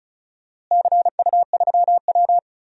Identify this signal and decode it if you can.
Morse code, heard as a keyed tone, about 34 wpm, 700 Hz. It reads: CU3W
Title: DXCC-100 Call Signs - 4 Characters - Repeated 3x - 35wpm